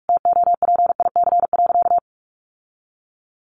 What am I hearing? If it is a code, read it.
Morse code, heard as a keyed tone, about 44 wpm, 700 Hz. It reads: TOPIC.
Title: Sentences from Top 1200 Words 45wpm